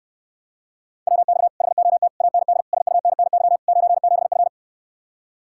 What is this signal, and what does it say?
Morse code, heard as a keyed tone, about 69 wpm, 700 Hz. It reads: WX HOT AND SUNNY 97F